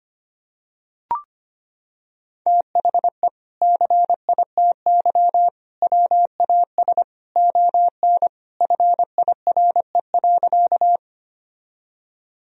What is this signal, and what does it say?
Morse code, heard as a keyed tone, about 25 wpm, 700 Hz. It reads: THE CITY WAS ON FIRE.
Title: Sentences from Top 1000 Words - Review Part 1 - 25wpm